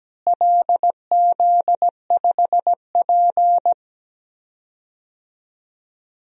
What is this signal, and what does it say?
Morse code, heard as a keyed tone, about 17 wpm, 700 Hz. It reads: LZ5P
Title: DXCC-100 Call Signs - 4 Characters - Repeated 2x - 17wpm